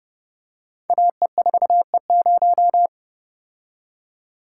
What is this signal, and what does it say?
Morse code, heard as a keyed tone, about 30 wpm, 700 Hz. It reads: AE4E0
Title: Code Groups - 5 Letter:Numbers - 30wpm